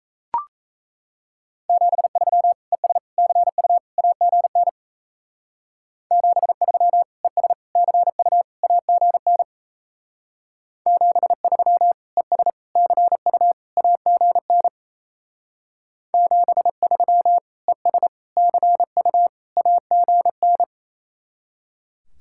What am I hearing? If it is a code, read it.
Morse code, heard as a keyed tone, about 35 wpm, 700 Hz. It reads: 73 ES CU AGN 73 ES CU AGN 73 ES CU AGN 73 ES CU AGN